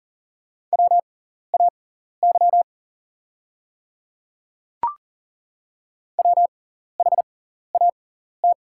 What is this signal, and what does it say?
Morse code, heard as a keyed tone, about 40 wpm, spaced as for 14 wpm, 700 Hz. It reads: WAY WHAT
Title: 6x Character Spacing - Top 100 Words - 40wpm